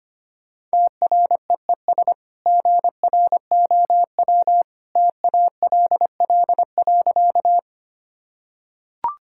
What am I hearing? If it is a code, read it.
Morse code, heard as a keyed tone, about 25 wpm, 690 Hz. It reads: TREES GROW TALL.
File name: Easy Sentences with Hints - 25wpm